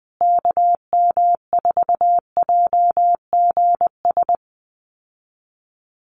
Morse code, heard as a keyed tone, about 20 wpm, 700 Hz. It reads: KM4JGS
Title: USA Call Signs Repeated 3x 20wpm